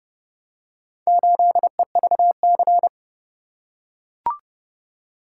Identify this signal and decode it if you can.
Morse code, heard as a keyed tone, about 29 wpm, 700 Hz. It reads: 8EVC